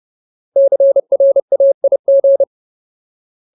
Morse code, heard as a keyed tone, about 30 wpm, 550 Hz. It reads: CRAIG
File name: Ultimate Word List / 30WPM Ultimate